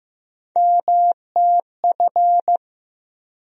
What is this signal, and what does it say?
Morse code, heard as a keyed tone, about 15 wpm, 700 Hz. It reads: MTF
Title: USA Call Sign Postfixes - 3 Letters 15wpm